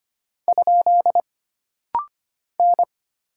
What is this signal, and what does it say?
Morse code, heard as a keyed tone, about 25 wpm, 700 Hz. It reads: ? N
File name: Single Character (Letter:Number:Punctuation:Prosign) - Rapid-Fire - 25wpm